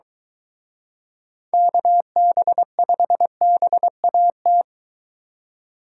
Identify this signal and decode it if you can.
Morse code, heard as a keyed tone, about 23 wpm, 700 Hz. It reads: KB5BAT